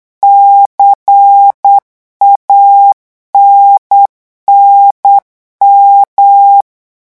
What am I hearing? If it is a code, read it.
Morse code, heard as a keyed tone, about 8 wpm, 800 Hz. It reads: CANNM
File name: morse mp3 / revision3-vitesse 8 mots